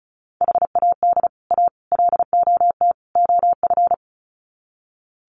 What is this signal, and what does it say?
Morse code, heard as a keyed tone, about 34 wpm, 700 Hz. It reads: HAD A LOT OF
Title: Sets of 4 Words 35wpm